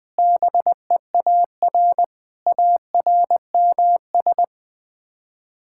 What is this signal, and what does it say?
Morse code, heard as a keyed tone, about 20 wpm, 700 Hz. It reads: BEAR ARMS
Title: Sets of 9 Words - 20wpm